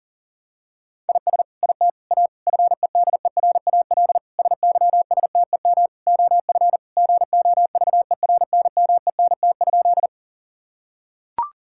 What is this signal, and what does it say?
Morse code, heard as a keyed tone, about 37 wpm, 700 Hz. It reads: IS IT A FEDERAL SYSTEM OF GOVERNMENT?
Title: Sentences from Top 1000 Words - Review Part 5 - 40wpm